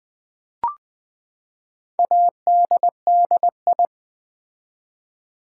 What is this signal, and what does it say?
Morse code, heard as a keyed tone, about 20 wpm, 700 Hz. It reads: ADDI
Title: Build A Word - 4 Letters At A Time - 10-Letter Words - 20wpm